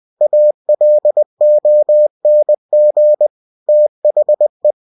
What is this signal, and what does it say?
Morse code, heard as a keyed tone, about 20 wpm, 600 Hz. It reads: ALONG THE
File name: winnie-20wpm